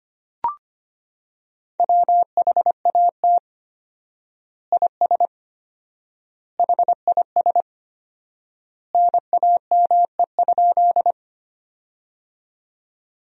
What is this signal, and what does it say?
Morse code, heard as a keyed tone, about 25 wpm, 700 Hz. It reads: WHAT IS HIS NAME?
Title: 4x Word-Spacing - Sentences from Top 200 Words - 25wpm